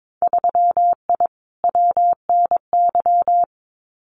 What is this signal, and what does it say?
Morse code, heard as a keyed tone, about 22 wpm, 700 Hz. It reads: 3I WNY
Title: Winter Field Day Exchanges 22wpm